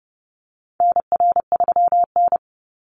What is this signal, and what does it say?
Morse code, heard as a keyed tone, about 30 wpm, 700 Hz. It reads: NR3N
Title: Call Signs - 30wpm